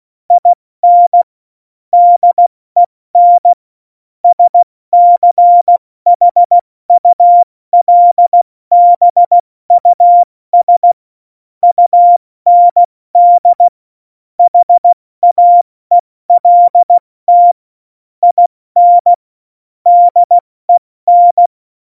Morse code, heard as a keyed tone, about 16 wpm, 700 Hz. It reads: IN DEN SCHULBUS UND HAELT IN DEN
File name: Morsekurs / Witze / Witz 12. Februar 2019, 70 F80